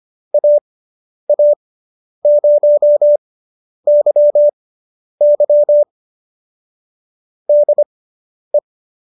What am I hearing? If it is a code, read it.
Morse code, heard as a keyed tone, about 25 wpm, spaced as for 10 wpm, 590 Hz. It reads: AA0YY DE